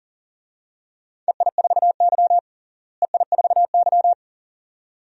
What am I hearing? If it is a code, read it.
Morse code, heard as a keyed tone, about 37 wpm, 700 Hz. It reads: EI4Y EI4Y
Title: DXCC-100 Call Signs - 4 Characters - Repeated 2x - 40wpm